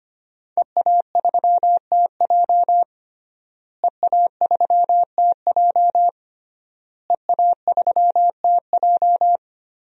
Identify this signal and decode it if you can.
Morse code, heard as a keyed tone, about 25 wpm, 700 Hz. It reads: EA3TJ EA3TJ EA3TJ
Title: DXCC-100 Call Signs - 5 Characters - Repeated 3x - 25wpm